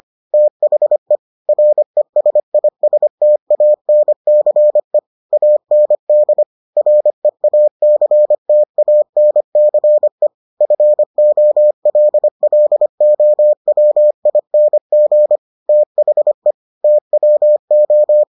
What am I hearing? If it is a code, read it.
Morse code, heard as a keyed tone, about 25 wpm, 600 Hz. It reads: THE RESISTANCE AND REACTANCE FOLLOWING THE TWO